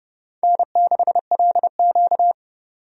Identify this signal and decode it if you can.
Morse code, heard as a keyed tone, about 30 wpm, 700 Hz. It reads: N6LQ
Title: USA Call Signs - 4 Characters - 30wpm